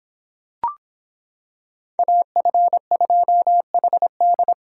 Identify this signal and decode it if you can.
Morse code, heard as a keyed tone, about 26 wpm, 700 Hz. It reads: AF2HD